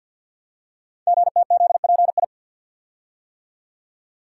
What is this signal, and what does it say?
Morse code, heard as a keyed tone, about 49 wpm, 700 Hz. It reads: GTZPI
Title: Code Groups - 5 Letters - 50wpm